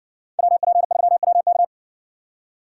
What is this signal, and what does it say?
Morse code, heard as a keyed tone, about 60 wpm, 700 Hz. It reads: WP3WK